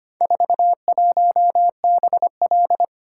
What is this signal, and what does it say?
Morse code, heard as a keyed tone, about 25 wpm, 700 Hz. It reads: 41BL
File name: practice / 25wpm / Four Letter-Number Combinations - 25wpm